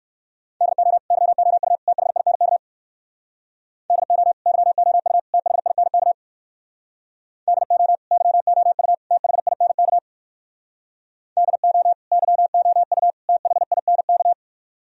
Morse code, heard as a keyed tone, about 58 wpm, 700 Hz. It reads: DO YOU THINK DO YOU THINK DO YOU THINK DO YOU THINK